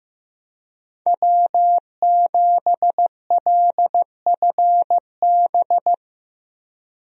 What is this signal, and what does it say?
Morse code, heard as a keyed tone, about 15 wpm, 700 Hz. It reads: W7LFB
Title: USA Call Signs - 5 Characters - Repeated 3x - 15wpm